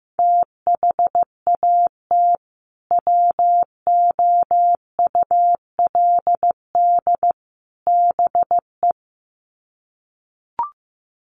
Morse code, heard as a keyed tone, about 15 wpm, 700 Hz. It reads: THAT WOULD BE
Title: Sets of 3 Words 15wpm